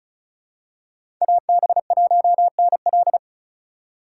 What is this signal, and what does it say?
Morse code, heard as a keyed tone, about 36 wpm, 700 Hz. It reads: AB1NL
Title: USA Call Signs - 5 Characters - 4x Character Spacing - 35wpm